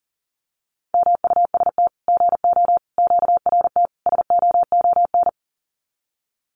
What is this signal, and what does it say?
Morse code, heard as a keyed tone, about 40 wpm, 700 Hz. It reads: MUST GO QRT SOON
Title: Common QSO Phrases 40wpm